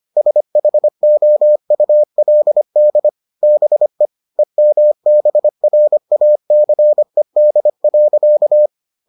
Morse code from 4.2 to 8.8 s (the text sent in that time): EMBRACED.